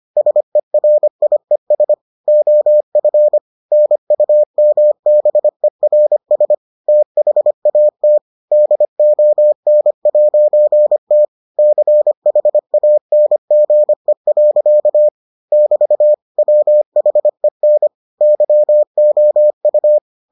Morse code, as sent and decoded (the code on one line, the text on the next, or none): ... .-. .. ... / --- ..-. / -. ..- -- -... .-. ... / - .... .- - / -.. --- -. .----. - / -.-. .... .- -. --. .-.-.- / -...- / .-- .... -. / -.-- --- ..-
SERIES OF NUMBERS THAT DON'T CHANGE. <BT> WHEN YOU